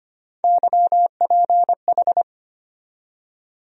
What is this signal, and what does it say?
Morse code, heard as a keyed tone, about 25 wpm, 700 Hz. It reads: YPH
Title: USA Call Sign Postfixes - 3 Letters 25wpm